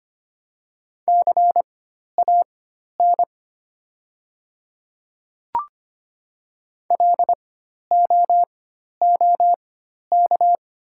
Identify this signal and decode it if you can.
Morse code, heard as a keyed tone, about 25 wpm, spaced as for 12 wpm, 690 Hz. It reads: CAN LOOK
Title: 4x Character Spacing - Top 100 Words - 25wpm